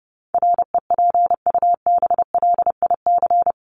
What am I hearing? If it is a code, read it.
Morse code, heard as a keyed tone, about 30 wpm, 700 Hz. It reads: REPUBLIC
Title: Top 1251-1500 Words 30wpm